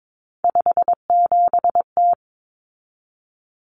Morse code, heard as a keyed tone, about 22 wpm, 700 Hz. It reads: 57T
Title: Common Cut Numbers - 3-Digit 22wpm